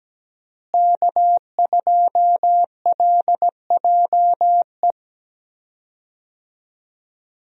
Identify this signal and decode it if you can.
Morse code, heard as a keyed tone, about 17 wpm, 700 Hz. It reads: K2LJE
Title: USA Call Signs - 5 Characters - Repeated 3x - 17wpm